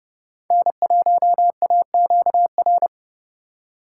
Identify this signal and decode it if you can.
Morse code, heard as a keyed tone, about 30 wpm, 700 Hz. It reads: N1AQR
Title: USA Call Signs - 5 Characters - 4x Character Spacing - 30wpm